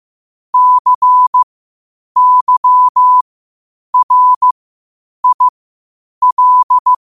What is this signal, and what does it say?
Morse code, heard as a keyed tone, about 15 wpm, spaced as for 8 wpm, 1000 Hz. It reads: CYRIL
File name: qotdfiles / 8wpm